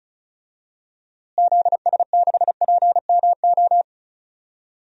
Morse code, heard as a keyed tone, about 34 wpm, 700 Hz. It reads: ZS6PMO
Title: DXCC-100 Call Signs - 6 Characters - Repeated 2x - 35wpm